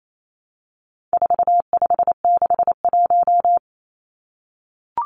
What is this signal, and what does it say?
Morse code, heard as a keyed tone, about 28 wpm, 700 Hz. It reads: 4561
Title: Numbers - 4-Digit 28wpm